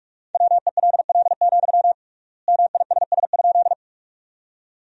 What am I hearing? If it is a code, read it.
Morse code, heard as a keyed tone, about 41 wpm, 700 Hz. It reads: WELL, MISS?
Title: 3x Word-Spacing - Sentences from Top 500 Words - 45wpm